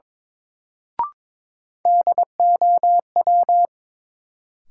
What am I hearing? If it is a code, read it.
Morse code, heard as a keyed tone, about 22 wpm, 700 Hz. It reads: DOW
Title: Sets of 5 Words - New Word Drill - 22wpm